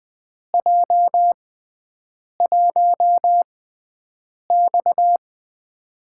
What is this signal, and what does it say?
Morse code, heard as a keyed tone, about 20 wpm, spaced as for 7 wpm, 700 Hz. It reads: J1X
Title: USA Call Signs - 6x Character Spacing - 20wpm